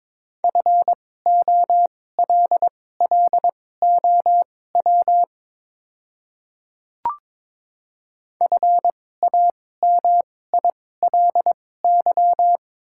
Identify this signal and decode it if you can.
Morse code, heard as a keyed tone, about 22 wpm, spaced as for 16 wpm, 690 Hz. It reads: FOLLOW FAMILY
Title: 2x Character Spacing - Top 500 Words - 22wpm